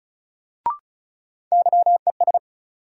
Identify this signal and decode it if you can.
Morse code, heard as a keyed tone, about 36 wpm, 700 Hz. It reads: YES
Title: Sentences from Top 500 Words - New Word Drill - 35wpm